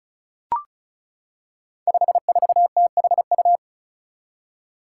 Morse code, heard as a keyed tone, about 36 wpm, 700 Hz. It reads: 54THU
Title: Five Challenging Letter-Number Combinations 35wpm